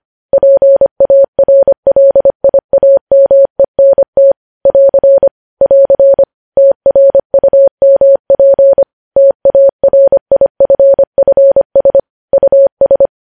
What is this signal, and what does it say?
Morse code, heard as a keyed tone, about 25 wpm, 550 Hz. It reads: PARLIAMENT <AR> <AR> TRUMP TARIFFS US
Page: news-headlines-in-morse-code-at-25-wpm